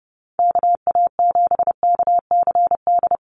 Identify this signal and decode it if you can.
Morse code, heard as a keyed tone, about 30 wpm, 700 Hz. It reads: KA7KCD
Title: USA Call Signs Repeated 3x 30wpm